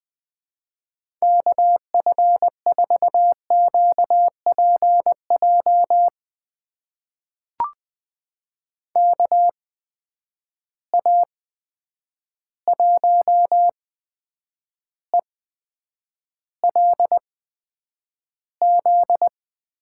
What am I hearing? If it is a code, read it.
Morse code, heard as a keyed tone, about 20 wpm, spaced as for 9 wpm, 700 Hz. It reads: KF4QPJ KA1ELZ
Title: USA Call Signs - 6 Characters - 8x Character Spacing - 20wpm